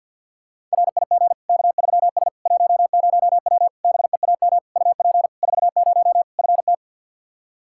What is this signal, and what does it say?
Morse code, heard as a keyed tone, about 47 wpm, 700 Hz. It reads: RIG K3S 10W BEAM UP 40 FT
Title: Practice QSO - Exchange 2 - Info - 50wpm